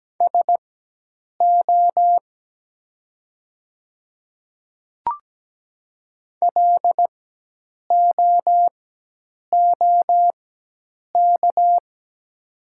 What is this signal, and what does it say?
Morse code, heard as a keyed tone, about 17 wpm, spaced as for 8 wpm, 700 Hz. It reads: SO LOOK
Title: 4x Character Spacing - Top 200 Words - 17wpm